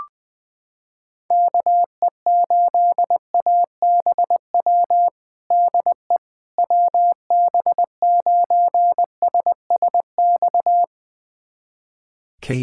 Morse code, heard as a keyed tone, about 19 wpm, 700 Hz. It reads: KE8ABW DE WB9SSX
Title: USA Callsign DE Callsign - 6 Characters - 20wpm